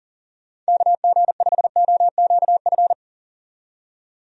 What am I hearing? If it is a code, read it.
Morse code, heard as a keyed tone, about 40 wpm, 700 Hz. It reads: KG5OQF
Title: USA Call Signs - 6 Characters - 40wpm